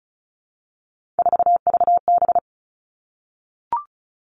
Morse code, heard as a keyed tone, about 36 wpm, 700 Hz. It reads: 4VB
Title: Three Challenging Letter-Number Combinations 35wpm